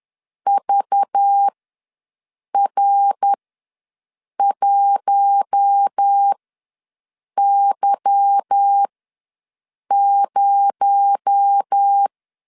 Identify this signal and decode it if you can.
Morse code, heard as a keyed tone, about 10 wpm, spaced as for 6 wpm, 800 Hz. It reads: VR1Y0
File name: LEZIONI / 8 YJ45 / 8 Y ws 10wpm 3952